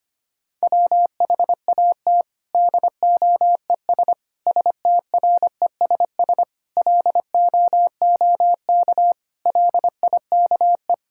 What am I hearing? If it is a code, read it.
Morse code, heard as a keyed tone, about 25 wpm, 700 Hz. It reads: WHAT DOES STRESS LOOK LIKE
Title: Sentences from Top 1000 Words - Review Part 6 - 25wpm